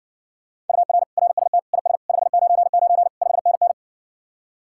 Morse code, heard as a keyed tone, about 61 wpm, 700 Hz. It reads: UR RST IS 599 5NN